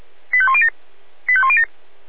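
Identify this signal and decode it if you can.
Five-tone selective call (ZVEI), heard as quick runs of five tones, tones of about 70 ms. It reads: 85107 85107
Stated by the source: ZVEI five-tone call received by SDR.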